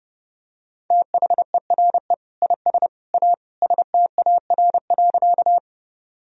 Morse code, heard as a keyed tone, about 30 wpm, 700 Hz. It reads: THERE IS A STAR.
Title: Sentences from Top 1000 Words - Review Part 2 - 30wpm